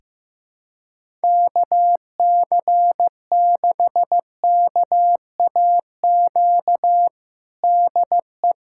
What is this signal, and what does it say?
Morse code, heard as a keyed tone, about 15 wpm, 700 Hz. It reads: KC6KAQ DE
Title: USA Callsign DE Callsign - 6 Characters - 15wpm